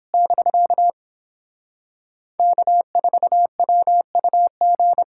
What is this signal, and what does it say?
Morse code, heard as a keyed tone, about 31 wpm, 700 Hz. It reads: <BK> K4WUG